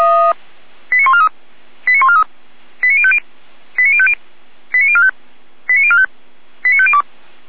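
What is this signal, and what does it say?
Five-tone selective call (ZVEI), heard as quick runs of five tones, tones of about 80 ms. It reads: 89143 89143 89050 89050 89045 89045 89502
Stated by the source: Five-tone ZVEI selective call received on an SDR, logged by feuerwehr-orlamuende.org (fire brigade Orlamünde).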